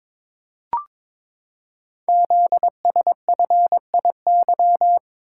Morse code, heard as a keyed tone, about 22 wpm, 700 Hz. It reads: ZSFIY
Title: Code Groups - 5 Letters - 22wpm